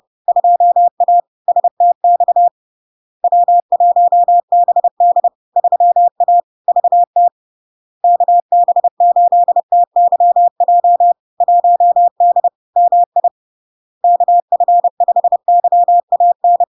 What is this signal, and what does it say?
Morse code, heard as a keyed tone, about 30 wpm, 700 Hz. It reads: 2A STX W1BD 3A VT KB8TYJ 1D MI KF5YAN
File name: LARK-FD-2022-Log-in-CW-30wpm-700-Hz